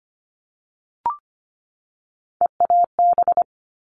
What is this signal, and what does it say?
Morse code, heard as a keyed tone, about 25 wpm, 700 Hz. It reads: EAB
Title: All Cut Numbers - 3-Digit 25wpm